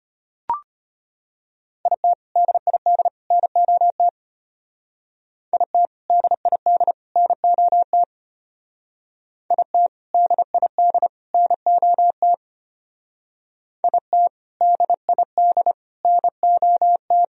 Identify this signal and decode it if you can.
Morse code, heard as a keyed tone, about 32 wpm, 700 Hz. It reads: IT DID NOT IT DID NOT IT DID NOT IT DID NOT